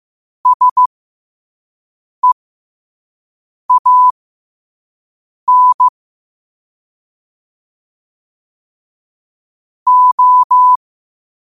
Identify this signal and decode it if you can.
Morse code, heard as a keyed tone, about 15 wpm, spaced as for 5 wpm, 1000 Hz. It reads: SEAN O
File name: qotdfiles / 5wpm